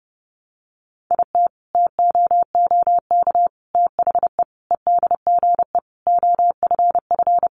Morse code, heard as a keyed tone, about 30 wpm, 700 Hz. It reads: IT TOOK THE EDGE OFF
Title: Sentences from Top 1300 Words 30wpm